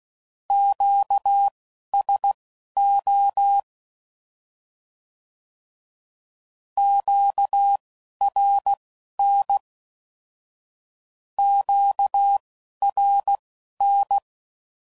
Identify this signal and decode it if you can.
Morse code, heard as a keyed tone, about 16 wpm, spaced as for 11 wpm, 800 Hz. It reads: QSO QRN QRN
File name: CW Code Groups @10WPM / Q Calls / Q Calls Twice